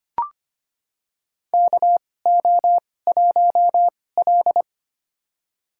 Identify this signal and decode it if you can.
Morse code, heard as a keyed tone, about 25 wpm, spaced as for 18 wpm, 700 Hz. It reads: KO1L
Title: USA Call Signs - 4 Characters - 2x Character Spacing - 25wpm